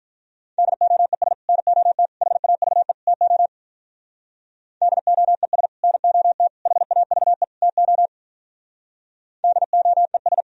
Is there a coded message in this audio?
Yes: Morse code from 0.6 to 10.4 s, heard as a keyed tone, about 44 wpm, 700 Hz. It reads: DOES NOT HAVE TO DOES NOT HAVE TO DOES